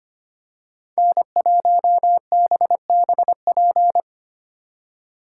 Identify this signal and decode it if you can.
Morse code, heard as a keyed tone, about 29 wpm, 700 Hz. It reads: N1BBP
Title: Speed Racing - USA 1x3 Call Signs 25wpm